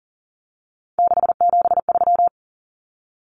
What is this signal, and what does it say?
Morse code, heard as a keyed tone, about 40 wpm, 700 Hz. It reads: 673
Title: Common Cut Numbers - 3-Digit 40wpm